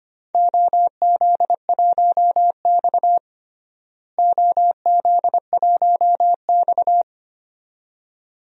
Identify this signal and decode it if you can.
Morse code, heard as a keyed tone, about 25 wpm, 700 Hz. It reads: OZ1X OZ1X
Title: DXCC-100 Call Signs - 4 Characters - Repeated 2x - 25wpm